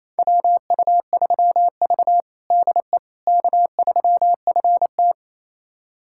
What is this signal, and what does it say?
Morse code, heard as a keyed tone, about 28 wpm, 700 Hz. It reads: WU3V DE K3FT
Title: USA Callsign DE Callsign - 4 Characters - 28wpm